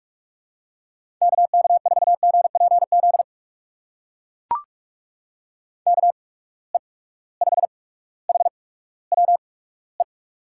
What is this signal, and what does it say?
Morse code, heard as a keyed tone, about 42 wpm, spaced as for 19 wpm, 700 Hz. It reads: KK4GPZ KE5HWE
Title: USA Call Signs - 6 Characters - 8x Character Spacing - 45wpm